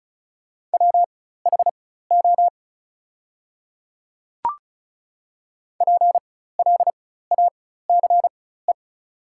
Morse code, heard as a keyed tone, about 36 wpm, spaced as for 16 wpm, 690 Hz. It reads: WHO PLACE
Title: 4x Character Spacing - Top 300 Words - 35wpm